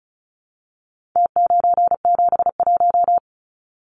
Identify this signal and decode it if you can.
Morse code, heard as a keyed tone, about 34 wpm, 700 Hz. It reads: T971
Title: Numbers - Contest Style - 4-Digit 35wpm